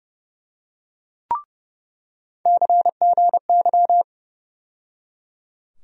Morse code, heard as a keyed tone, about 30 wpm, 700 Hz. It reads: CGY
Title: USA Call Sign Postfixes - 3 Letters 30wpm